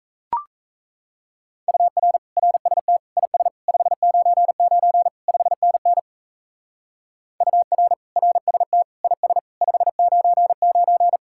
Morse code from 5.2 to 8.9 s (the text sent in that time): 5NN UR RST